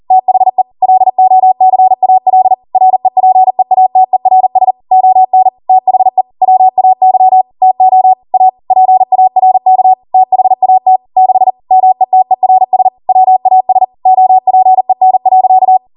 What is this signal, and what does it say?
Morse code, heard as a keyed tone, about 40 wpm, 750 Hz. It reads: THE LOCAL REPEATERS ON THE WAY TO A PARK THAT 6 METERS WAS OPEN.